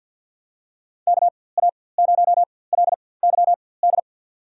Morse code, heard as a keyed tone, about 52 wpm, spaced as for 23 wpm, 700 Hz. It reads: KA0LYD
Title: USA Call Signs - 4x Character Spacing - 50wpm